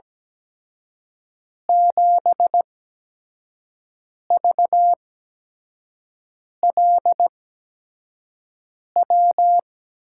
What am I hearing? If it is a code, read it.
Morse code, heard as a keyed tone, about 17 wpm, spaced as for 5 wpm, 700 Hz. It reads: 7VLW
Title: USA Call Signs - 6 Characters - 8x Character Spacing - 17wpm